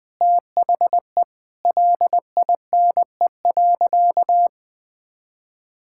Morse code, heard as a keyed tone, about 19 wpm, 700 Hz. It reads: THE LINE.
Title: Sentences with 5-Word Phrases - Spoken Beginning - 20wpm